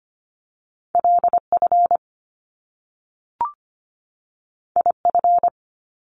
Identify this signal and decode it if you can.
Morse code, heard as a keyed tone, about 25 wpm, 700 Hz. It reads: LF IF
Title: Top Two Letter Combinations in Words 25wpm